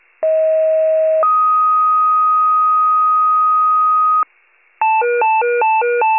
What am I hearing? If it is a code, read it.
Two-tone page, heard as a one-second tone, then a three-second tone, about 640 then 1250 Hz. It is heard from 0.2 to 4.2 s.
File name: pager / 38 FIRE / wcalert